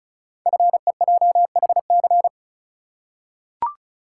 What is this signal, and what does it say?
Morse code, heard as a keyed tone, about 36 wpm, 700 Hz. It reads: FEJHC
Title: Code Groups - 5 Letters - 35wpm